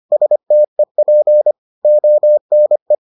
Morse code, heard as a keyed tone, about 25 wpm, 600 Hz. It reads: STEP ONE